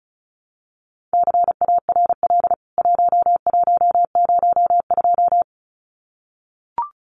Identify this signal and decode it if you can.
Morse code, heard as a keyed tone, about 34 wpm, 700 Hz. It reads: CARL 1102
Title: CWT Exchange 35wpm